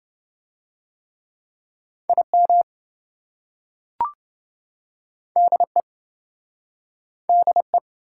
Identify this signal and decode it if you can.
Morse code, heard as a keyed tone, about 27 wpm, 700 Hz. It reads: IM DE DE